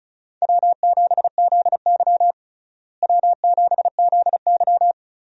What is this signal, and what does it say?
Morse code, heard as a keyed tone, about 36 wpm, 700 Hz. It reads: W7ZY W7ZY
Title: USA Call Signs - 4 Characters - Repeated 3x - 35wpm